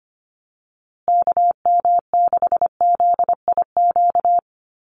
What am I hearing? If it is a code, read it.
Morse code, heard as a keyed tone, about 25 wpm, 700 Hz. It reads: KM6ZIQ
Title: USA Call Signs Repeated 2x 25wpm